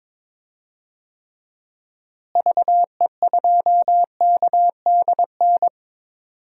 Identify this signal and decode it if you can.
Morse code, heard as a keyed tone, about 22 wpm, 700 Hz. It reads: VE2KDN
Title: DXCC-100 Call Signs - 6 Characters - Repeated 3x - 22wpm